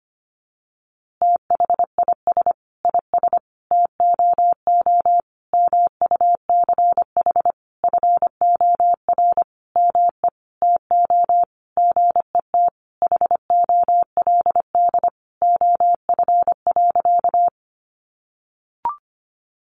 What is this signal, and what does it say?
Morse code, heard as a keyed tone, about 25 wpm, 700 Hz. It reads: THIS IS TOO MUCH FOR ME TO GET HOLD OF.
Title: A Sound Of Thunder 25wpm